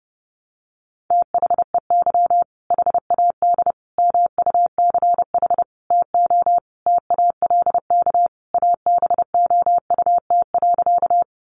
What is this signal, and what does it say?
Morse code, heard as a keyed tone, about 30 wpm, 700 Hz. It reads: THEY HAD MUCH TO TALK ABOUT.
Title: Sentences from Top 400 Words 30wpm